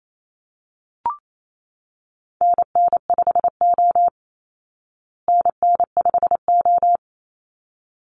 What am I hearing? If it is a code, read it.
Morse code, heard as a keyed tone, about 28 wpm, 700 Hz. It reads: NN5O NN5O
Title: Call Signs Repeated 3x - 28wpm